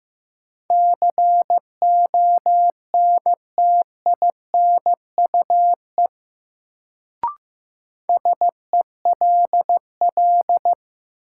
Morse code, heard as a keyed tone, about 15 wpm, 700 Hz. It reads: CONTINUE SELL
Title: Sentences from Top 1000 Words - Review Part 6 - New Word Drill - 15wpm